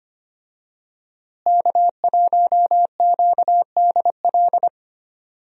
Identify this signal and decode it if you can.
Morse code, heard as a keyed tone, about 25 wpm, 700 Hz. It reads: K1QDL
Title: USA Call Signs - 5 Characters - Repeated 3x - 25wpm